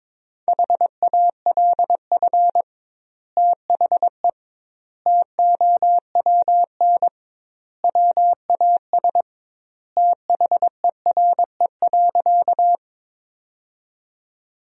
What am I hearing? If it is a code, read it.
Morse code, heard as a keyed tone, about 22 wpm, 700 Hz. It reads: HALF THE TOWN WAS THERE.
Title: 2x Word-Spacing - Sentences from Top 500 Words - 22wpm